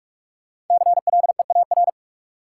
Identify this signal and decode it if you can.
Morse code, heard as a keyed tone, about 41 wpm, 700 Hz. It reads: CLEAR
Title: Sentences from Top 500 Words - New Word Drill - 45wpm